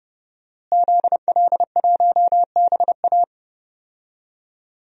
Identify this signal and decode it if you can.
Morse code, heard as a keyed tone, about 30 wpm, 700 Hz. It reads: ZL1BA
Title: DXCC-100 Call Signs - 5 Characters - Repeated 2x - 30wpm